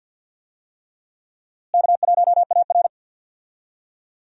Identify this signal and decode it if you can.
Morse code, heard as a keyed tone, about 50 wpm, 700 Hz. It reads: K1AR
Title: USA Call Signs - 4 Characters - Repeated 3x - 50wpm